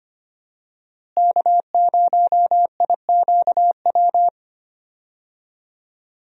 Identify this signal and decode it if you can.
Morse code, heard as a keyed tone, about 25 wpm, 700 Hz. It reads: K0IQW
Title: USA Call Signs - 5 Characters - Repeated 2x - 25wpm